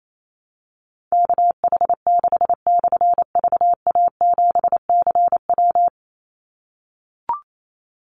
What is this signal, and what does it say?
Morse code, heard as a keyed tone, about 28 wpm, 700 Hz. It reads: KH6/VA7CW
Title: Call Signs - 28wpm